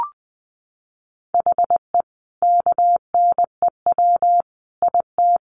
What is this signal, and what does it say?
Morse code, heard as a keyed tone, about 20 wpm, 700 Hz. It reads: HE KNEW IT
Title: Sentences from Top 1100 Words 20wpm